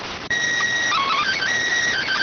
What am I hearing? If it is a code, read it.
SSTV (slow-scan television), PD 290.